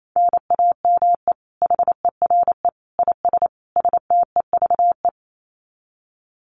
Morse code, heard as a keyed tone, about 28 wpm, 700 Hz. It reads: NAME HERE IS STEVE
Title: Common QSO Phrases 28wpm